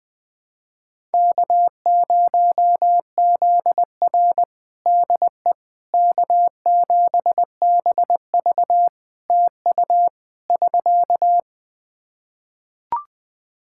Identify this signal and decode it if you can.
Morse code, heard as a keyed tone, about 21 wpm, 700 Hz. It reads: K0ZR DE K7BV TU <SK>
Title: Practice QSO - Exchange 4 - Ending - 20wpm